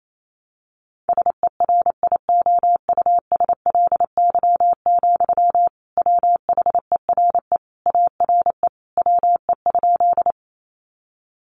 Morse code, heard as a keyed tone, about 28 wpm, 700 Hz. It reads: SERIOUSLY, WHERE ARE WE?
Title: Sentences from Top 1200 Words 28wpm